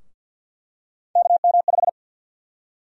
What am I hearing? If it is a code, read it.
Morse code, heard as a keyed tone, about 52 wpm, 700 Hz. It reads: KM5
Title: USA Call Sign Prefixes - 50wpm